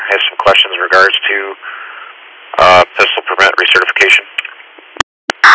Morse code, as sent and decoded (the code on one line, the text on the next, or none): none